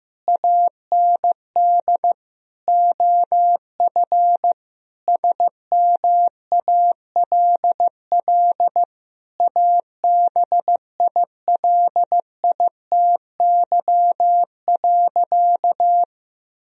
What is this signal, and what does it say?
Morse code, heard as a keyed tone, about 15 wpm, 700 Hz. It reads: AND OF SMALL ABILITY.
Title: Sentences from Top 1000 Words - Review Part 3 - 15wpm